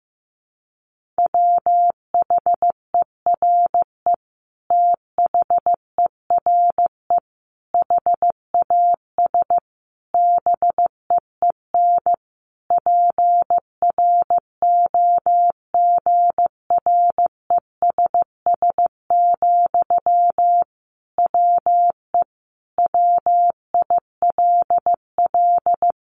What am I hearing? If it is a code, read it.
Morse code, heard as a keyed tone, about 15 wpm, 700 Hz. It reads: WHERE THERE HAS BEEN PROGRESS, WE WILL
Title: Sentences from Top 1400 Words 15wpm